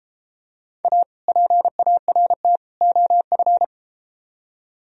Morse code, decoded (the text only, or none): A PART OF